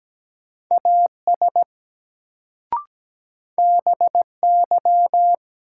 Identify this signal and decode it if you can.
Morse code, heard as a keyed tone, about 17 wpm, 700 Hz. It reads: AS BY
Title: Sentences from Top 100 Words - New Word Drill - 17wpm